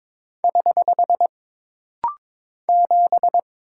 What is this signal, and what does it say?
Morse code, heard as a keyed tone, about 22 wpm, 700 Hz. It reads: <HH> 7
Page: This rapid-fire timing is geared to push your ICR (Instant Character Recognition) skills.